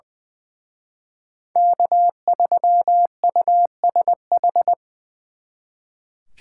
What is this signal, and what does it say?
Morse code, heard as a keyed tone, about 25 wpm, 700 Hz. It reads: K3USH
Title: Speed Racing - USA 1x3 Call Signs 20wpm